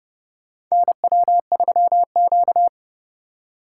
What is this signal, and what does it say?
Morse code, heard as a keyed tone, about 30 wpm, 700 Hz. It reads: NW3Q
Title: USA Call Signs - 4 Characters - 2x Character Spacing - 30wpm